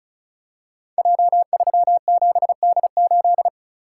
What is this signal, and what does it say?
Morse code, heard as a keyed tone, about 36 wpm, 700 Hz. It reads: J37D8
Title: Code Groups - 5 Letter:Numbers - 35wpm